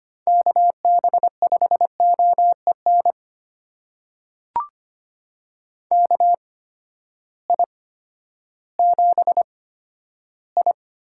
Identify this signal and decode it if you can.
Morse code, heard as a keyed tone, about 25 wpm, spaced as for 12 wpm, 700 Hz. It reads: KB5OEN KI7I
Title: USA Call Signs - 6 Characters - 8x Character Spacing - 25wpm